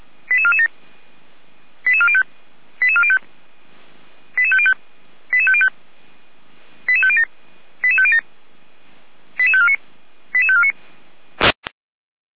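Five-tone selective call (ZVEI), heard as quick runs of five tones, tones of about 70 ms. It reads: 88447 88405 88405 88505 88505 88507 88507 88549 88549
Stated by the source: ZVEI five-tone selective call received on an SDR.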